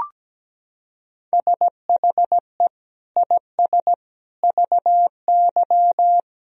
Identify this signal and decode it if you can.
Morse code, heard as a keyed tone, about 17 wpm, 700 Hz. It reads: SHE IS VY
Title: Sentences from Top 400 Words and Abbreviations 17wpm